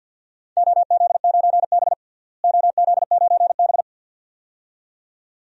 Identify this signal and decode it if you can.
Morse code, heard as a keyed tone, about 52 wpm, 700 Hz. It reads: OZ9B OZ9B
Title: DXCC-100 Call Signs - 4 Characters - Repeated 2x - 50wpm